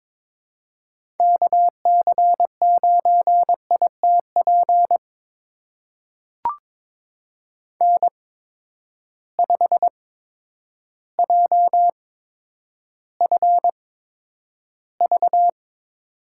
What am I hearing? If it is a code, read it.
Morse code, heard as a keyed tone, about 22 wpm, spaced as for 10 wpm, 700 Hz. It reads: KC9ITP N5JFV
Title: USA Call Signs - 8x Character Spacing - 22wpm